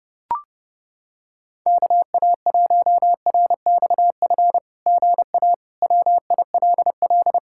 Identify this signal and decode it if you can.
Morse code, heard as a keyed tone, about 30 wpm, 700 Hz. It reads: KA1RXF GA WILL
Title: SOTA QSO Practice 30wpm